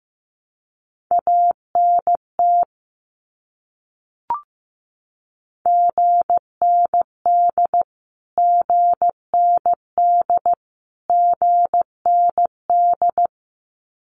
Morse code, heard as a keyed tone, about 15 wpm, 700 Hz. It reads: ANT GND GND GND
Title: Abbreviations Repeated 3x - 15wpm